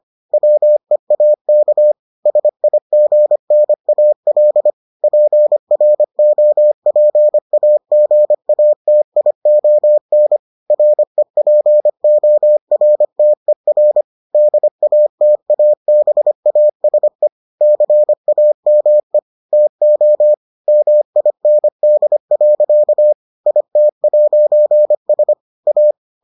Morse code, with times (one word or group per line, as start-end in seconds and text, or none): 0.3-1.9 WEAK
2.3-4.7 SIGNAL
5.0-10.4 PROPAGATION
10.7-14.0 REPORTER
14.3-17.3 DATABASE
17.6-19.2 CAME
19.5-20.3 TO
20.7-23.1 MIND.
23.5-25.3 IT'S
25.7-25.9 A